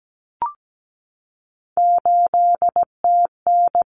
Morse code, heard as a keyed tone, about 17 wpm, 700 Hz. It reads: 8TN
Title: Common Cut Numbers - 3-Digit 17wpm